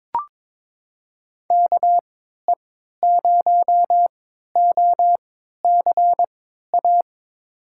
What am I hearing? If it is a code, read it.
Morse code, heard as a keyed tone, about 22 wpm, spaced as for 13 wpm, 700 Hz. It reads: KE0OCA
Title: USA Call Signs - 6 Characters - 3x Character Spacing - 22wpm